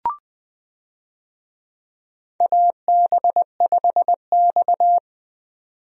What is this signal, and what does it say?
Morse code, heard as a keyed tone, about 20 wpm, 700 Hz. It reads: AB5X
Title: USA Call Signs - 4 Characters - 20wpm